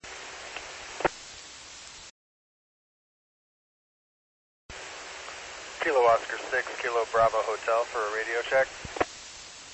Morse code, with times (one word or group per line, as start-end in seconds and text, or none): none